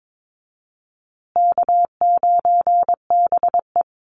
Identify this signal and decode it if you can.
Morse code, heard as a keyed tone, about 22 wpm, 700 Hz. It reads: K9BE
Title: USA Call Signs Repeated 2x 22wpm